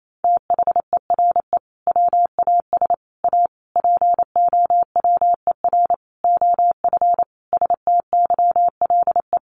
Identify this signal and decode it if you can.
Morse code, heard as a keyed tone, about 28 wpm, 700 Hz. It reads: THERE WAS A POWER OF STYLE
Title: Sentences from Top 800 Words 28wpm